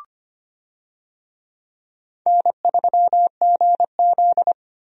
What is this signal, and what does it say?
Morse code, heard as a keyed tone, about 25 wpm, 700 Hz. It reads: N3GZ
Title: USA Call Signs - 4 Characters - Repeated 2x - 25wpm